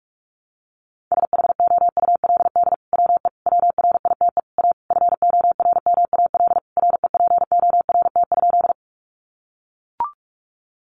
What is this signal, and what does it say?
Morse code, heard as a keyed tone, about 44 wpm, 700 Hz. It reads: SHOULD WE WRITE A FORMAL REPORT?
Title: Sentences from Top 1300 Words 45wpm